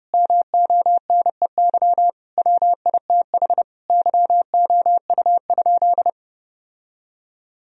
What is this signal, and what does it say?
Morse code, heard as a keyed tone, about 32 wpm, 700 Hz. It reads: MONEY WITH YOU?
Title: Sentences from Top 1000 Words - Review Part 1 - 30wpm